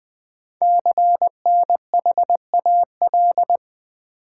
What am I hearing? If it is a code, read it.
Morse code, heard as a keyed tone, about 20 wpm, 700 Hz. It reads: CNHAL
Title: Code Groups - 5 Letters - 20wpm